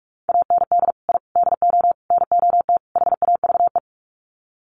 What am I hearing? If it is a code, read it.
Morse code, heard as a keyed tone, about 44 wpm, 700 Hz. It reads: AND I DO NOT HAVE
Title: Sets of 5 Words Encore 45wpm